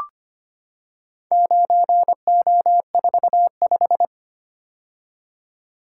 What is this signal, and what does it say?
Morse code, heard as a keyed tone, about 25 wpm, 700 Hz. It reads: 9O45
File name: practice / 25wpm / Four Letter-Number Combinations - 25wpm